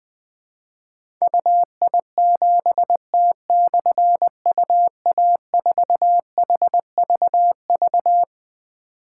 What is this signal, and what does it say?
Morse code, heard as a keyed tone, about 20 wpm, 700 Hz. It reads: UI7T/UA4HVV
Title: Difficult Call Signs 20wpm